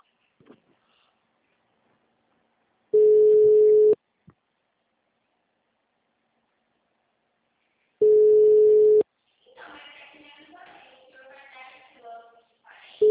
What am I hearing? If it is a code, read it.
Ringback tone (Europe, 425 Hz).